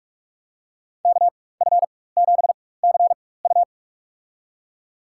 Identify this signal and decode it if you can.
Morse code, heard as a keyed tone, about 46 wpm, spaced as for 21 wpm, 700 Hz. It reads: KF7CU
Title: USA Call Signs - 4x Character Spacing - 45wpm